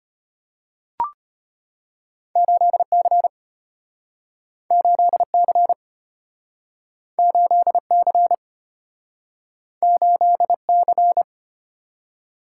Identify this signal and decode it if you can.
Morse code, heard as a keyed tone, about 34 wpm, 700 Hz. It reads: 8C 8C 8C 8C